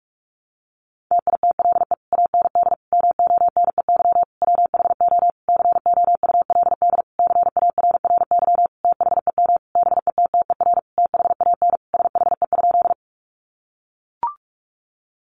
Morse code, heard as a keyed tone, about 44 wpm, 700 Hz. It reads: TITLE AND MONEY WHO COULD CARRY THEM BETTER THAN SHE?
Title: Sentences from Top 1200 Words 45wpm